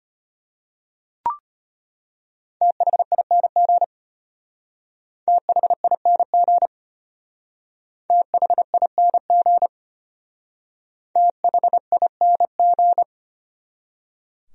Morse code, heard as a keyed tone, about 31 wpm, 700 Hz. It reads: THING THING THING THING
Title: Speed-Racing Top 100 Words 25wpm